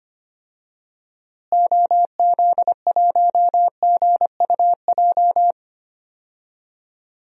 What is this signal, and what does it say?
Morse code, heard as a keyed tone, about 25 wpm, 700 Hz. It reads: OZ1GUJ
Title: DXCC-100 Call Signs - 6 Characters - Repeated 3x - 25wpm